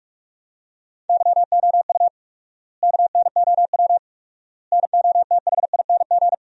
Morse code, heard as a keyed tone, about 41 wpm, 700 Hz. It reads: YOU KNOW NOTHING
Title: 4x Word-Spacing - Sentences from Top 500 Words - 45wpm